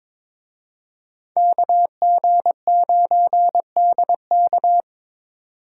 Morse code, heard as a keyed tone, about 26 wpm, 700 Hz. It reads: KG9DK
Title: Speed Racing - USA 2x2 Call Signs 22wpm